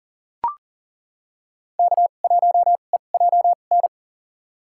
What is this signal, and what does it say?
Morse code, heard as a keyed tone, about 40 wpm, spaced as for 29 wpm, 700 Hz. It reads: K1EJN
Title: USA Call Signs - 5 Characters - 2x Character Spacing - 40wpm